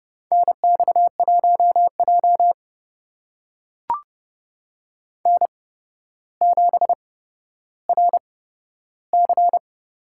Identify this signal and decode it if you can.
Morse code, heard as a keyed tone, about 30 wpm, spaced as for 13 wpm, 700 Hz. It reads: NX1J N7RC
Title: USA Call Signs - 4 Characters - 8x Character Spacing - 30wpm